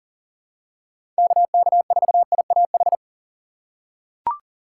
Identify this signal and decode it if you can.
Morse code, heard as a keyed tone, about 40 wpm, 700 Hz. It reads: KK4IAH
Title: USA Call Signs - 6 Characters - 4x Character Spacing - 40wpm